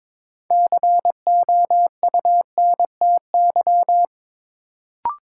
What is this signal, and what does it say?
Morse code, heard as a keyed tone, about 22 wpm, 700 Hz. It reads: COUNTY
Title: Sentences from Top 1200 Words - New Word Drill Alt - 22wpm